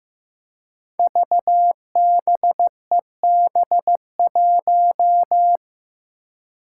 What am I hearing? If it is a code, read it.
Morse code, heard as a keyed tone, about 15 wpm, 700 Hz. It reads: VBEB1
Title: Five Challenging Letter-Number Combinations 15wpm